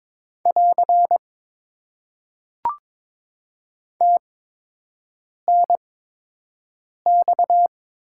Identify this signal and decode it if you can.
Morse code, heard as a keyed tone, about 23 wpm, spaced as for 6 wpm, 700 Hz. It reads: <AR> TNX
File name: Top 90 QSO Elements - 8x Character Spacing - 22wpm